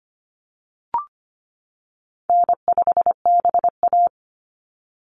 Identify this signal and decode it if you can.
Morse code, heard as a keyed tone, about 25 wpm, 700 Hz. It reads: N5BA
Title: Call Signs - 25wpm